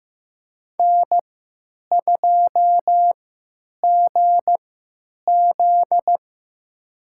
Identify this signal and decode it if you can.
Morse code, heard as a keyed tone, about 15 wpm, spaced as for 9 wpm, 700 Hz. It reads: N2GZ
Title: USA Call Signs - 4 Characters - 3x Character Spacing - 15wpm